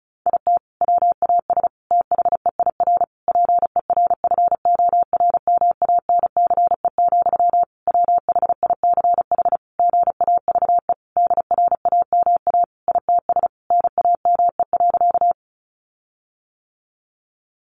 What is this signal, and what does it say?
Morse code, heard as a keyed tone, about 34 wpm, 700 Hz. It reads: IT WAS THEIR PERFORMANCE, WHICH GAVE DRAMA ITS NAME.
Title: Sentences from Top 1500 Words 35wpm